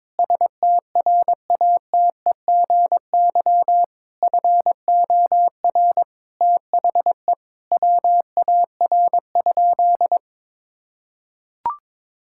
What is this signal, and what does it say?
Morse code, heard as a keyed tone, about 22 wpm, 700 Hz. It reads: STRATEGY FOR THE WAR?
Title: Sentences from Top 1000 Words - Review Part 5 - 22wpm